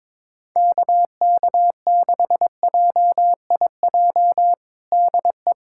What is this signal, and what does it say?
Morse code, heard as a keyed tone, about 22 wpm, 700 Hz. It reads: KK6JIJ DE
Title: USA Callsign DE Callsign - 6 Characters - 22wpm